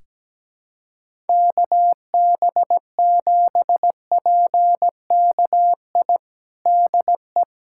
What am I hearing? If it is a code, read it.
Morse code, heard as a keyed tone, about 17 wpm, 700 Hz. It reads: KB7PKI DE
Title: USA Callsign DE Callsign - 6 Characters - 17wpm